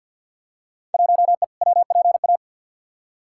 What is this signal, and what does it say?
Morse code, heard as a keyed tone, about 53 wpm, 700 Hz. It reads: 1E WPA
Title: Field Day Exchanges - 50wpm